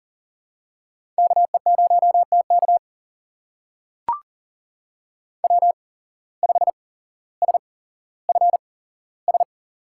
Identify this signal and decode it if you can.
Morse code, heard as a keyed tone, about 40 wpm, spaced as for 17 wpm, 700 Hz. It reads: KE0TK W5SFS
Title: USA Call Signs - 5 Characters - 8x Character Spacing - 40wpm